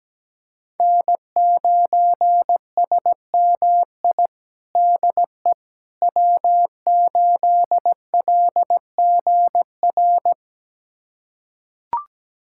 Morse code, heard as a keyed tone, about 17 wpm, 700 Hz. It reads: N9SMI DE W8LGR
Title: USA Callsign DE Callsign - 5 Characters - 17wpm